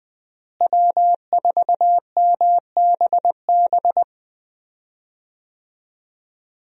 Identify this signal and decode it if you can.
Morse code, heard as a keyed tone, about 21 wpm, 700 Hz. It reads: W4MBB
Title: USA Call Signs - 5 Characters - Repeated 2x - 20wpm